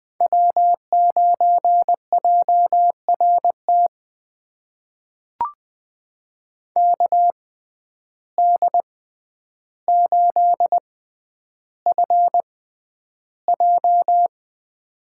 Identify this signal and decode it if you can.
Morse code, heard as a keyed tone, about 20 wpm, spaced as for 10 wpm, 700 Hz. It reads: W9JRT KD8FJ
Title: USA Call Signs - 5 Characters - 6x Character Spacing - 20wpm